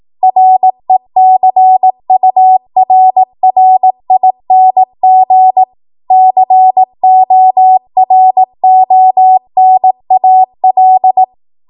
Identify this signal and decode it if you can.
Morse code, heard as a keyed tone, about 18 wpm, 750 Hz. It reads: RECURRING CORONAL
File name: Morse / 18WPM